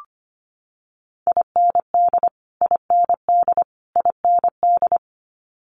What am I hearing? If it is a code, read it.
Morse code, heard as a keyed tone, about 25 wpm, 700 Hz. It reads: IND IND IND
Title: Multi-letter Phonemes Repeated 3x 25wpm